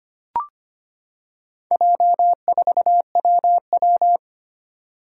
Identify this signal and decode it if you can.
Morse code, heard as a keyed tone, about 25 wpm, 700 Hz. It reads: J4WW
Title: Four Challenging Letter-Number Combinations 25wpm